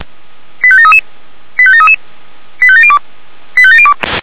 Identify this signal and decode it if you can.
Five-tone selective call (ZVEI), heard as quick runs of five tones, tones of about 70 ms. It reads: 85733 85733 85702 85702